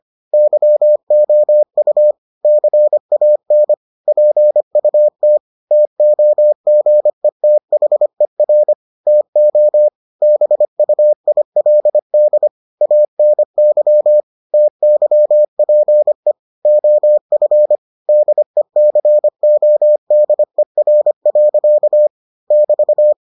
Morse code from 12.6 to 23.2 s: ANY TYPE OF DECODER. <BT>

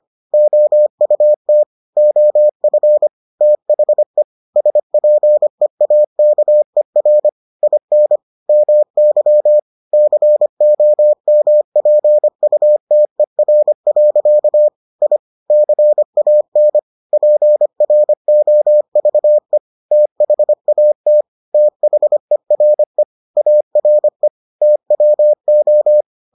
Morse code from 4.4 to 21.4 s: SPEAKER IN MY COMPUTER. I CAN PROVE THAT

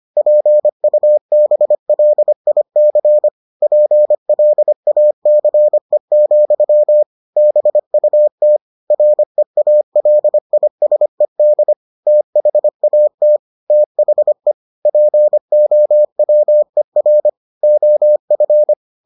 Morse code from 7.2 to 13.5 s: BUT REALISED THAT

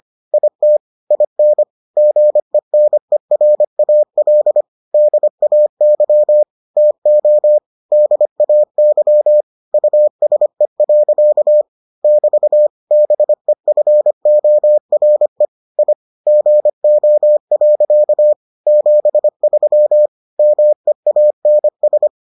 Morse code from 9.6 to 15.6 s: USE. <BT> BEFORE

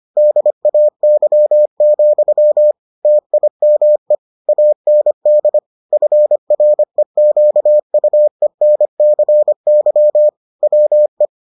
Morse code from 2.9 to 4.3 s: TIME

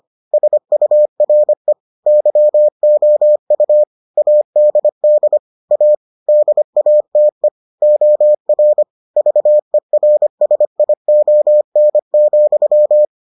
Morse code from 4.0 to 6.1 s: ADD A

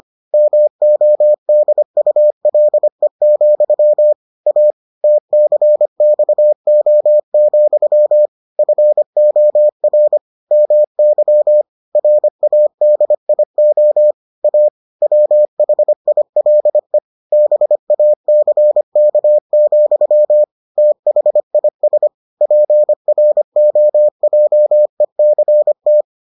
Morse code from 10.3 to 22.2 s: MY RADIO A WHILE BACK, THIS

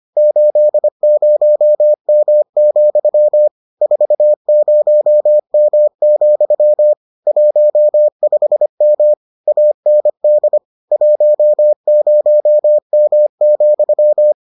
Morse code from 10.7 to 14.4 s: 10M,

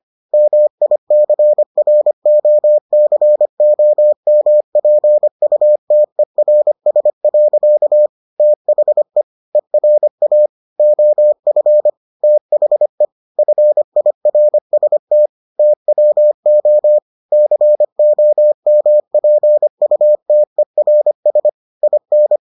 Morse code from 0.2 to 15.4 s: MICROCOMPUTERS. THE ERA OF THE FIRST

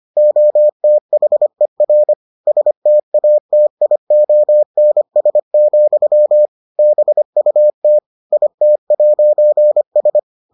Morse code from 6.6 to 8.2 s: BUT

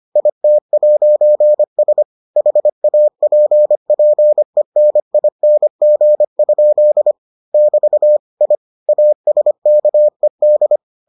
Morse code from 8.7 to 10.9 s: ASKED

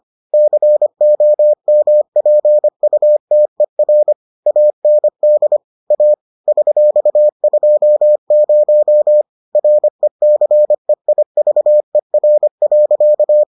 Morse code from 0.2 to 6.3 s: COMPUTER AND A

